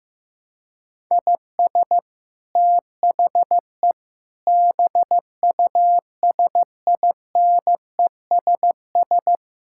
Morse code, heard as a keyed tone, about 15 wpm, 700 Hz. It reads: IS THE BUSINESS
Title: Sentences from Top 1000 Words - Review Part 6 - 15wpm